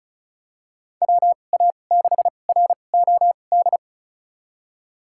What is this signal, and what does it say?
Morse code, heard as a keyed tone, about 36 wpm, spaced as for 25 wpm, 700 Hz. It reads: WA6ROD
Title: USA Call Signs - 6 Characters - 2x Character Spacing - 35wpm